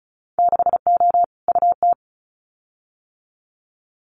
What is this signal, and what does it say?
Morse code, heard as a keyed tone, about 36 wpm, 700 Hz. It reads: 6O UT
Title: Winter Field Day Exchanges Encore 35wpm